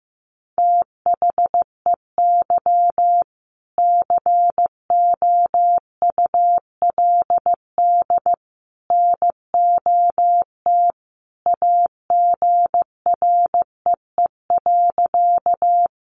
Morse code, heard as a keyed tone, about 15 wpm, 700 Hz. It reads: THEY COULD NOT AGREE.
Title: Sentences from Top 1100 Words 15wpm